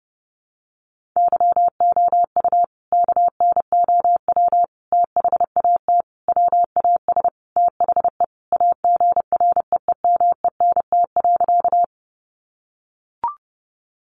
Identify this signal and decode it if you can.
Morse code, heard as a keyed tone, about 30 wpm, 700 Hz. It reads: YOU KNOW THAT WAS THE AGREEMENT.
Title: Sentences from Top 1500 Words 30wpm